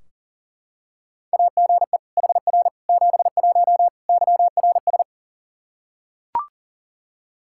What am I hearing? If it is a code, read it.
Morse code, heard as a keyed tone, about 40 wpm, 690 Hz. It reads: AGE HR 71 YRS
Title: Practice QSO - Exchange 3 - More Info - 40wpm